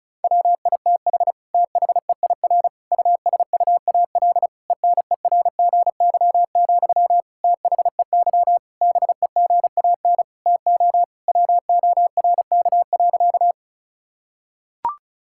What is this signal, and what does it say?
Morse code, heard as a keyed tone, about 36 wpm, 700 Hz. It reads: WITH THEIR USUAL ENERGY, THEY BEGAN TO WORK.
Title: Sentences from Top 1000 Words - Review Part 6 - 35wpm